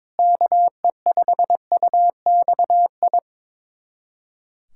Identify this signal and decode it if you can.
Morse code, heard as a keyed tone, about 22 wpm, 700 Hz. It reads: KE5UXI
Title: USA Callsign DE Callsign 22wpm